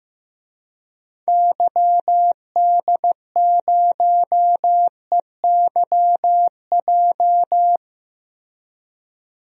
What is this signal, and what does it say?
Morse code, heard as a keyed tone, about 15 wpm, 700 Hz. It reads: YD0EYJ
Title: DXCC-100 Call Signs - 6 Characters - Repeated 2x - 15wpm